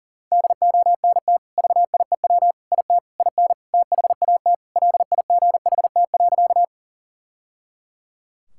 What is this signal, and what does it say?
Morse code, heard as a keyed tone, about 37 wpm, 700 Hz. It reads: DONT VIEW IT IN THAT LIGHT.
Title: Sentences from Top 1000 Words - Review Part 6 - 40wpm